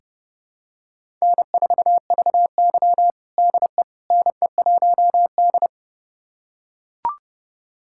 Morse code, heard as a keyed tone, about 32 wpm, 700 Hz. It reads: N4VY DE NE1D
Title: USA Callsign DE Callsign - 4 Characters - 30wpm